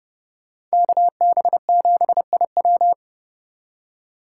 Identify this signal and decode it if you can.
Morse code, heard as a keyed tone, about 36 wpm, 700 Hz. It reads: KB7IW
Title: Speed Racing - USA 2x2 Call Signs 30wpm